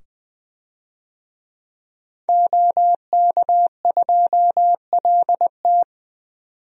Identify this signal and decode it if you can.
Morse code, heard as a keyed tone, about 20 wpm, 700 Hz. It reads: OK2LT
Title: DXCC-100 Call Signs - 5 Characters - 20wpm